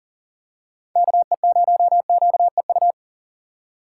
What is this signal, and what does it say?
Morse code, heard as a keyed tone, about 40 wpm, 700 Hz. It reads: KE0QEU
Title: USA Call Signs - 3x Character Spacing - 40wpm